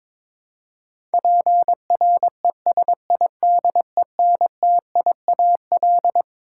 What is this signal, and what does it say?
Morse code, heard as a keyed tone, about 22 wpm, 700 Hz. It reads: PRESIDENTIAL
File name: Twelve Letter Words - 22wpm